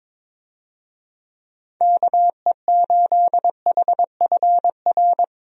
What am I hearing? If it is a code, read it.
Morse code, heard as a keyed tone, about 22 wpm, 700 Hz. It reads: KE8HFR
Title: USA Call Signs - 6 Characters - 22wpm